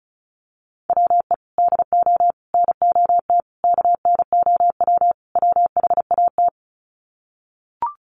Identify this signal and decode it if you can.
Morse code, heard as a keyed tone, about 34 wpm, 700 Hz. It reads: WE DO NOT KNOW WHAT
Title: Sets of 5 Words 35wpm